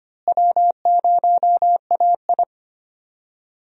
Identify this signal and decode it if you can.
Morse code, heard as a keyed tone, about 25 wpm, 700 Hz. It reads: W0AI
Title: USA Call Signs - 4 Characters - Repeated 2x - 25wpm